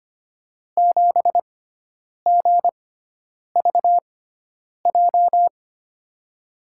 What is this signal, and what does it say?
Morse code, heard as a keyed tone, about 25 wpm, spaced as for 9 wpm, 700 Hz. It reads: 7GVJ
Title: USA Call Signs - 6x Character Spacing - 25wpm